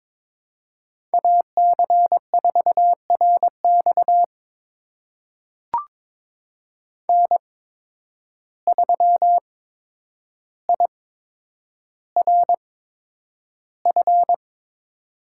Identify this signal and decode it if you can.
Morse code, heard as a keyed tone, about 23 wpm, spaced as for 10 wpm, 700 Hz. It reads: AC4RX N3IRF
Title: USA Call Signs - 8x Character Spacing - 22wpm